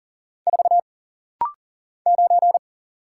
Morse code, heard as a keyed tone, about 39 wpm, 700 Hz. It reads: 4 9
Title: Single Letter-Number - Rapid-Fire 40wpm